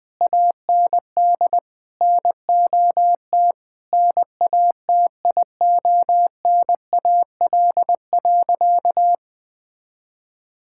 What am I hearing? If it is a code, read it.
Morse code, heard as a keyed tone, about 20 wpm, 700 Hz. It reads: AND NOT NATIONAL.
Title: Sentences from Top 1000 Words - Review Part 5 - 20wpm